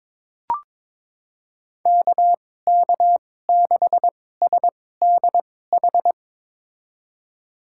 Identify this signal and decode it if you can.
Morse code, heard as a keyed tone, about 22 wpm, spaced as for 16 wpm, 700 Hz. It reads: KK6SDH
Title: USA Call Signs - 6 Characters - 2x Character Spacing - 22wpm